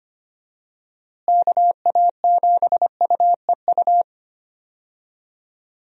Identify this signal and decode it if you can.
Morse code, heard as a keyed tone, about 25 wpm, 700 Hz. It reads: KA7UEU
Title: USA Call Signs - 6 Characters - Repeated 2x - 25wpm